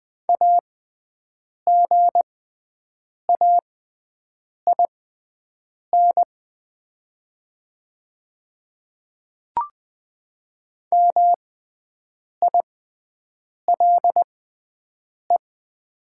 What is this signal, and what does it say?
Morse code, heard as a keyed tone, about 20 wpm, spaced as for 7 wpm, 700 Hz. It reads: AGAIN MILE
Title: 6x Character Spacing - Top 500 Words - 20wpm